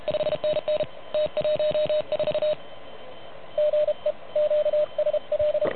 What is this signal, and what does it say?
Morse code, heard as a keyed tone, about 31 wpm, 600 Hz, with radio noise. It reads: 5NN T14 GE QSL
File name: truSDX / MWC29.8